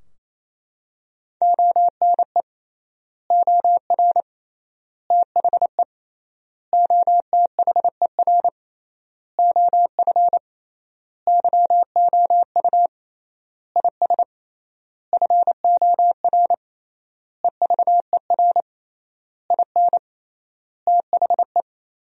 Morse code, heard as a keyed tone, about 28 wpm, 700 Hz. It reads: ONE OR THE OTHER OF YOU IS FOR EVER IN THE
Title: 3x Word-Spacing - Sentences from Top 400 Words - 28wpm